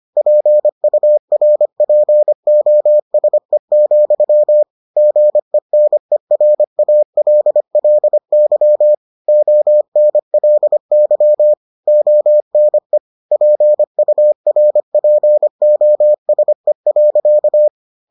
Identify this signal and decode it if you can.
Morse code, heard as a keyed tone, about 25 wpm, 600 Hz. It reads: PURPOSE, GENERALLY ONLY ONE PURPOSE.